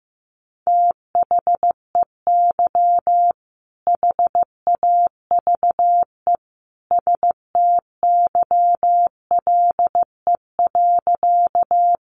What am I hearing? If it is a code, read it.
Morse code, heard as a keyed tone, about 15 wpm, 700 Hz. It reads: THEY HAVE STYLE.
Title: Sentences from Top 800 Words 15wpm